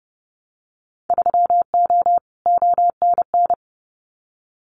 Morse code, heard as a keyed tone, about 30 wpm, 700 Hz. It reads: 3O ONN
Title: Winter Field Day Exchanges 30wpm